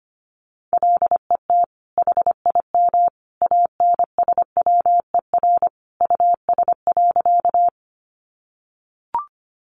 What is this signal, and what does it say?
Morse code, heard as a keyed tone, about 25 wpm, 700 Hz. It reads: LET HIM ANSWER US.
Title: Sentences from Top 300 Words 25wpm